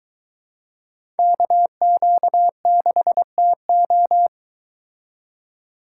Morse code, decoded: KQ6TO

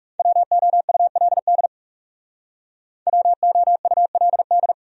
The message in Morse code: WOULD WOULD

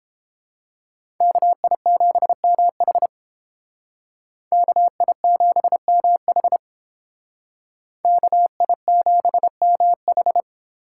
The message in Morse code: KI7MH KI7MH KI7MH

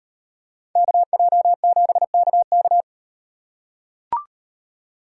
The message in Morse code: KJ7KK

